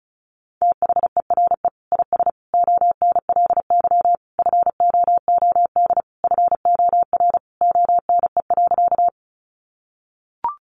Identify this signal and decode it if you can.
Morse code, heard as a keyed tone, about 34 wpm, 700 Hz. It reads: THERE IS ONLY FOOD FOR ONE.
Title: Sentences from Top 300 Words 35wpm